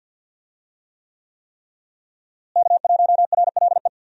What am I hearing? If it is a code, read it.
Morse code, heard as a keyed tone, about 50 wpm, 700 Hz. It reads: K1RLE
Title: USA Call Signs - 5 Characters - Repeated 3x - 50wpm